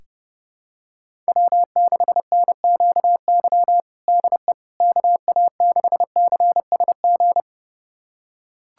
Morse code, heard as a keyed tone, about 31 wpm, 700 Hz. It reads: W6NQY DE KA6CSG
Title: USA Callsign DE Callsign 30wpm